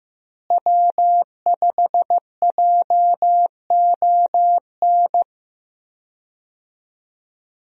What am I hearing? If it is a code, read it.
Morse code, heard as a keyed tone, about 15 wpm, 700 Hz. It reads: W5JON
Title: USA Call Signs - 5 Characters - Repeated 2x - 15wpm